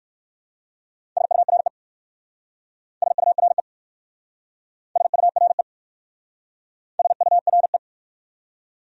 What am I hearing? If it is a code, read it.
Morse code, heard as a keyed tone, about 59 wpm, 700 Hz. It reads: SURE SURE SURE SURE